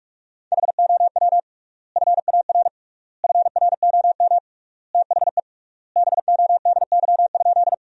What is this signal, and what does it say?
Morse code, heard as a keyed tone, about 41 wpm, 700 Hz. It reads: HOW FAR FROM THE BODY?
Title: 3x Word-Spacing - Sentences from Top 400 Words - 45wpm